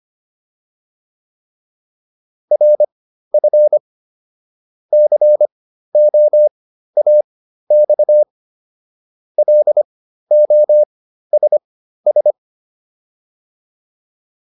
Morse code, heard as a keyed tone, about 25 wpm, spaced as for 13 wpm, 600 Hz. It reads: RF COAX LOSS